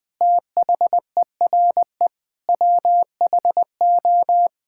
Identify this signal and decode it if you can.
Morse code, heard as a keyed tone, about 20 wpm, 700 Hz. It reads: THERE WHO
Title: Sentences with 9-Word Phrases - 20wpm